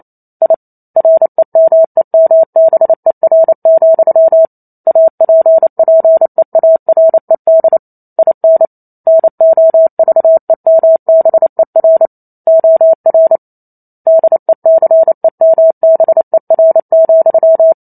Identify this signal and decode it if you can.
Morse code, heard as a keyed tone, about 28 wpm, 650 Hz. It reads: I REMEMBER, APPEARED IN NOVEMBER OR DECEMBER,